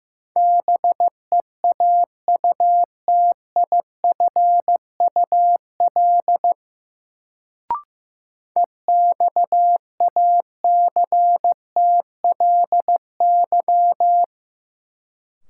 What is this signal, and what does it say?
Morse code, heard as a keyed tone, about 15 wpm, 700 Hz. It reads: BEAUTIFUL EXACTLY
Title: Sentences from Top 900 Words - New Word Drill - 15wpm